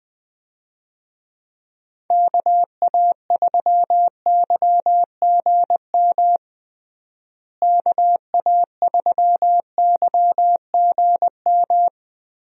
Morse code, heard as a keyed tone, about 20 wpm, 700 Hz. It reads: KA3YGM KA3YGM
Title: USA Call Signs - 6 Characters - Repeated 3x - 20wpm